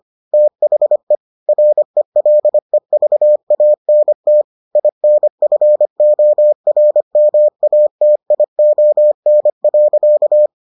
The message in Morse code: THE RELEVANT INFORMATION.

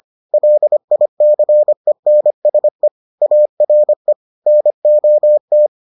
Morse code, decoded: LICENSE ARE NOT

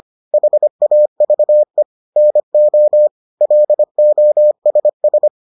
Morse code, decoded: HAVE NO LOSS

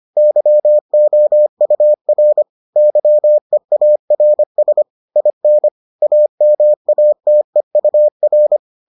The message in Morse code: YOUR YEARS IN AMATEUR

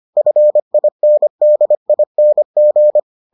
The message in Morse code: FINDING

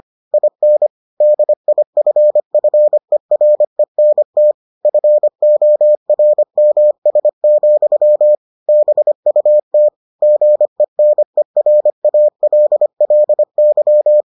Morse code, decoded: IN DIFFERENT FORMS, BUT GENERALLY

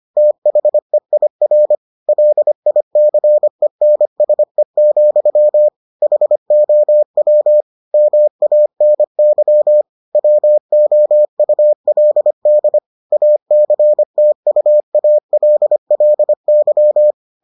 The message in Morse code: THEIR LICENSE, HOW MANY WOULD ACTUALLY

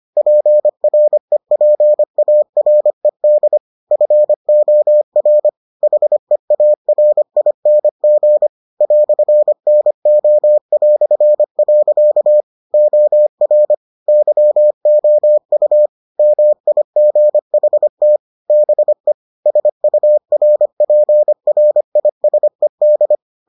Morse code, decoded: PREPARED FOR HEARING "NO". OR YOU MIGHT BE SURPRISED

